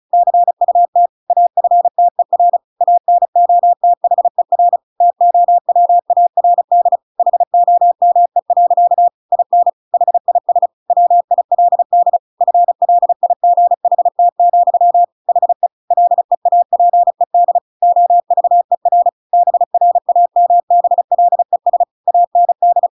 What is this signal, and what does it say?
Morse code, heard as a keyed tone, about 34 wpm, 700 Hz. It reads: CUT AFTER ANOTHER TOWARD HOME. IN HIS WILD FLIGHT, HE LEAPED OVER BRAMBLES AND